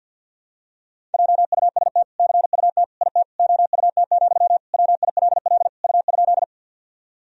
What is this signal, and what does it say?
Morse code, heard as a keyed tone, about 46 wpm, 700 Hz. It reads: JUST CUT IT OUT, WILL U?
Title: Sentences from Top 300 Words and Abbreviations 50wpm